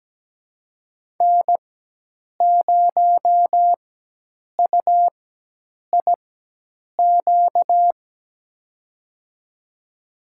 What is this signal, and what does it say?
Morse code, heard as a keyed tone, about 17 wpm, spaced as for 8 wpm, 700 Hz. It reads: N0UIQ
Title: USA Call Signs - 5 Characters - 4x Character Spacing - 17wpm